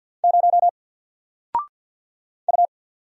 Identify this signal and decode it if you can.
Morse code, heard as a keyed tone, about 54 wpm, 700 Hz. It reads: 0 U
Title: Single Letter-Number Alt 50wpm